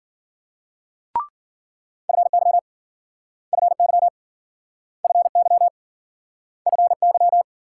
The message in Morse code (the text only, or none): FY FY FY FY